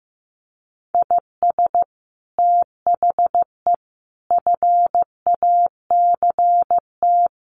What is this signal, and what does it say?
Morse code, heard as a keyed tone, about 15 wpm, 700 Hz. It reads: IS THE FACT
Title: Sets of 5 Words Encore2 15wpm